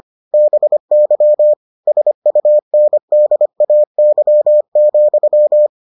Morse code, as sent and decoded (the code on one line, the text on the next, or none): -... -.-- / ... ..- -. -.. .- -.-- --..--
BY SUNDAY,